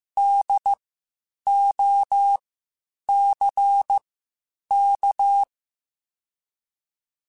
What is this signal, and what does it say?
Morse code, heard as a keyed tone, about 15 wpm, spaced as for 8 wpm, 800 Hz. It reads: DOCK